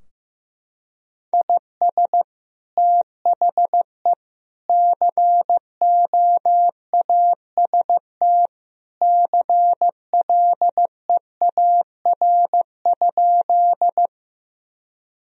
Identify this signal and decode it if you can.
Morse code, heard as a keyed tone, about 16 wpm, 700 Hz. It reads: IS THE COAST CLEAR?
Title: Sentences from Top 1000 Words - Review Part 4 - 15wpm